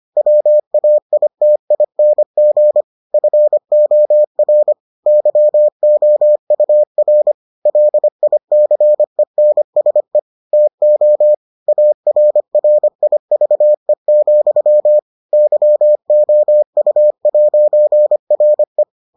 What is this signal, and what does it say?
Morse code, heard as a keyed tone, about 25 wpm, 600 Hz. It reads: WAITING FOR YOUR LICENSE TO ARRIVE, YOU'RE